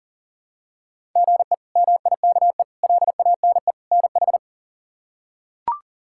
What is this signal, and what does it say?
Morse code, heard as a keyed tone, about 40 wpm, 700 Hz. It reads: GE MIKE LANE NH
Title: K1USN SST QSO Practice - 40wpm